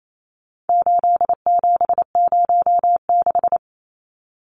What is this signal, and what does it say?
Morse code, heard as a keyed tone, about 28 wpm, 700 Hz. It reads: 8706
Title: Numbers - 4-Digit 28wpm